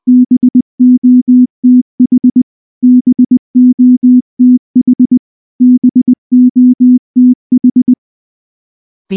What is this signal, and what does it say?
Morse code, heard as a keyed tone, about 19 wpm, 250 Hz. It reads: BOTH BOTH BOTH
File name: morse / morse-training-20WPM-10